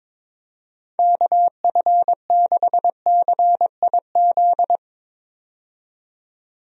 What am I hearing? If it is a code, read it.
Morse code, heard as a keyed tone, about 22 wpm, 700 Hz. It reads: KF6CIZ
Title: USA Call Signs - 6 Characters - 22wpm